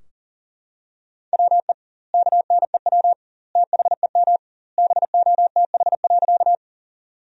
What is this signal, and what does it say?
Morse code, heard as a keyed tone, about 40 wpm, 700 Hz. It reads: WE KNEW THEM BOTH.
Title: 2x Word-Spacing - Sentences from Top 400 Words - 40wpm